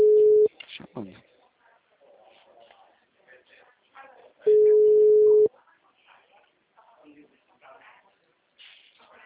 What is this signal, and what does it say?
Ringback tone (Europe, 425 Hz), with speech.